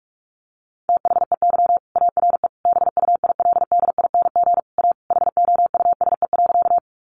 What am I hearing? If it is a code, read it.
Morse code, heard as a keyed tone, about 44 wpm, 700 Hz. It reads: THEY ARE BUILDING A HOUSE.
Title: Sentences from Top 700 Words 45wpm